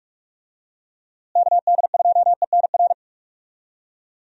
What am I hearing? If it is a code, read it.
Morse code, heard as a keyed tone, about 46 wpm, 700 Hz. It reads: KD2ENR
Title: USA Call Signs - 6 Characters - Repeated 2x - 45wpm